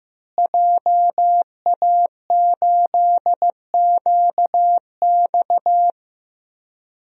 Morse code, heard as a keyed tone, about 15 wpm, 700 Hz. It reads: JA8QX
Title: DXCC-100 Call Signs - 5 Characters - Repeated 2x - 15wpm